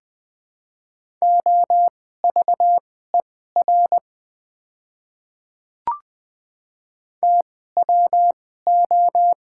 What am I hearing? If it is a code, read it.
Morse code, heard as a keyed tone, about 20 wpm, spaced as for 14 wpm, 700 Hz. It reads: OVER TWO
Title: 2x Character Spacing - Top 300 Words - 20wpm